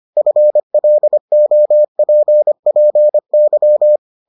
Morse code, heard as a keyed tone, about 25 wpm, 600 Hz. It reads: FLOPPY